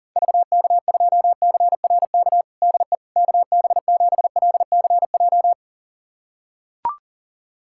Morse code, heard as a keyed tone, about 40 wpm, 700 Hz. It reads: VK2CRK DE KB7LCJ
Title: USA Callsign DE Callsign 40wpm